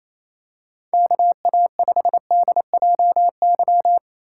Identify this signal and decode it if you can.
Morse code, heard as a keyed tone, about 28 wpm, 700 Hz. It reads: KA5DJY
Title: USA Call Signs - 6 Characters - 2x Character Spacing - 28wpm